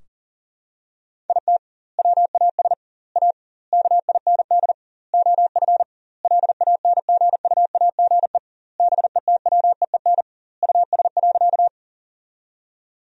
Morse code, heard as a keyed tone, about 40 wpm, 700 Hz. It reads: IT WAS A KIND OF LANGUAGE BETWEEN US.
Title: 2x Word-Spacing - Sentences from Top 500 Words - 40wpm